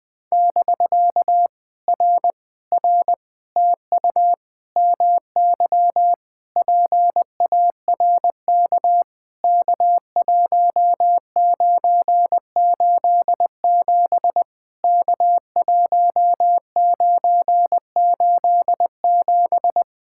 Morse code, heard as a keyed tone, about 20 wpm, 690 Hz. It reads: <BK> R R TU MY PARK K1987 K1987
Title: POTA QSO Practice - 20wpm